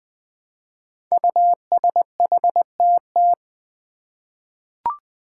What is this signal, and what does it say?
Morse code, heard as a keyed tone, about 20 wpm, 700 Hz. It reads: USHTT
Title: Five Challenging Letter-Number Combinations 20wpm